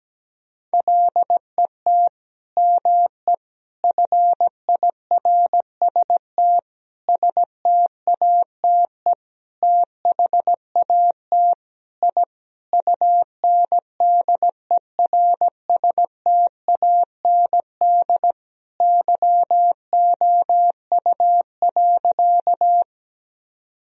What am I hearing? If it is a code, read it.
Morse code, heard as a keyed tone, about 17 wpm, 700 Hz. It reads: LET ME FIRST STATE THAT I UNDERSTAND YOU.
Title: Sentences from Top 1000 Words - Review Part 2 - 17wpm